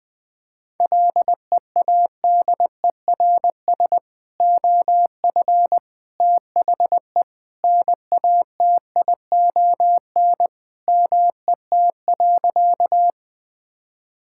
Morse code, heard as a keyed tone, about 21 wpm, 700 Hz. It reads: LEADERS OF THE NATION MET.
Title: Sentences with 3-Word Phrases - 20wpm